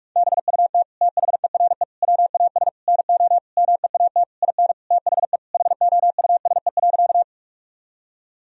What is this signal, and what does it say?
Morse code, heard as a keyed tone, about 42 wpm, 700 Hz. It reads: BUT THERE WAS NO MEAT IN THE HOUSE.
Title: Sentences from Top 1000 Words - Review Part 3 - 45wpm